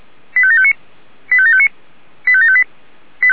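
Five-tone selective call (ZVEI), heard as quick runs of five tones, tones of about 70 ms. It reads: 85759 85759 85758
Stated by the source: ZVEI five-tone selective call received on SDR.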